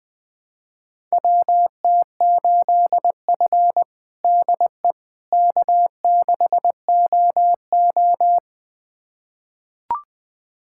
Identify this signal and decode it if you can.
Morse code, heard as a keyed tone, about 21 wpm, 700 Hz. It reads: WT8F DE K6OO
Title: USA Callsign DE Callsign - 4 Characters - 20wpm